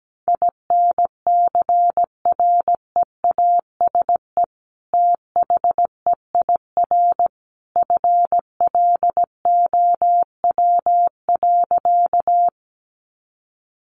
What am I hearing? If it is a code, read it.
Morse code, heard as a keyed tone, about 17 wpm, 700 Hz. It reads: INCREASE THEIR FLOW.
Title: Sentences from Top 1200 Words 17wpm